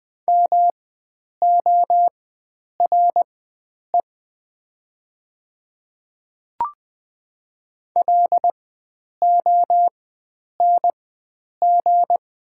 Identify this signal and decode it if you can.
Morse code, heard as a keyed tone, about 20 wpm, spaced as for 9 wpm, 700 Hz. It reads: MORE LONG
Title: 4x Character Spacing - Top 100 Words - 20wpm